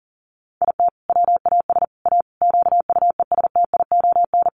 Morse code, heard as a keyed tone, about 37 wpm, 700 Hz. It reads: IT WAS A QUESTION
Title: Sentences from Top 1400 Words 40wpm